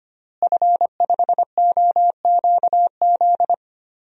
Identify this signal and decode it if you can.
Morse code, heard as a keyed tone, about 25 wpm, 700 Hz. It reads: F5OQZ
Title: DXCC-100 Call Signs - 5 Characters - Repeated 2x - 25wpm